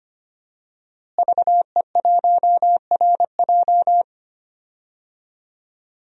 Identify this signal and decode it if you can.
Morse code, heard as a keyed tone, about 25 wpm, 700 Hz. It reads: VE1RJ
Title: DXCC-100 Call Signs - 5 Characters - Repeated 3x - 25wpm